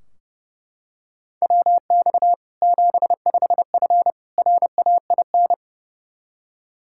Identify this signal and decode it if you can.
Morse code, heard as a keyed tone, about 30 wpm, 700 Hz. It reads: WX 75F RAIN
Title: Practice QSO - Exchange 2 - Info - 30wpm